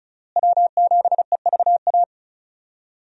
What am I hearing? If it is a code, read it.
Morse code, heard as a keyed tone, about 34 wpm, 700 Hz. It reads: W7EVA
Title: USA Call Signs - 5 Characters - Repeated 3x - 35wpm